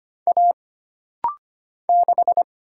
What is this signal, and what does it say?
Morse code, heard as a keyed tone, about 25 wpm, 700 Hz. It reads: A 6
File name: Single Character (Letter:Number:Punctuation:Prosign) - Rapid-Fire - 25wpm